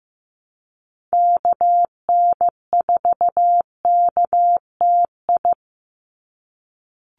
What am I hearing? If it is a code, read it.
Morse code, heard as a keyed tone, about 15 wpm, 700 Hz. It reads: KN4KTI
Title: USA Call Signs 15wpm